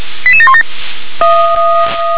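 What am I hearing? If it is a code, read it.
Five-tone selective call (ZVEI), heard as quick runs of five tones, tones of about 70 ms. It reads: 88717
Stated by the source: Five-tone ZVEI selective call received on SDR.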